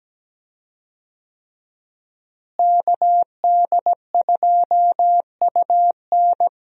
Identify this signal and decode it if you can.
Morse code, heard as a keyed tone, about 17 wpm, 700 Hz. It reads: KD2UN
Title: USA Call Signs - 5 Characters - Repeated 3x - 17wpm